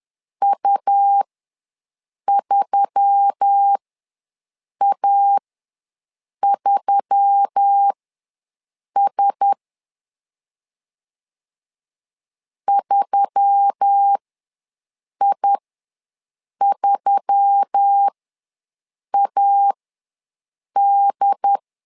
Morse code, heard as a keyed tone, about 10 wpm, spaced as for 6 wpm, 800 Hz. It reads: U3A3S 3I3AD
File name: LEZIONI / 4 Q1S3 ws 10wpm 2437